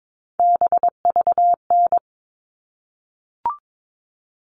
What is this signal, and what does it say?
Morse code, heard as a keyed tone, about 22 wpm, 700 Hz. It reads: BVN
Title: All Cut Numbers - 3-Digit 22wpm